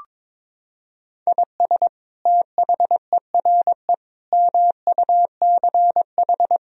Morse code, heard as a keyed tone, about 22 wpm, 700 Hz. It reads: IS THERE MUCH
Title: Sentences from Top 1000 Words - Review Part 4 - 22wpm